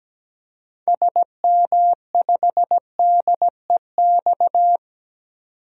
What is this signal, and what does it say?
Morse code, heard as a keyed tone, about 17 wpm, 700 Hz. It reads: SM5DEX
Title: DXCC-100 Call Signs - 6 Characters - 17wpm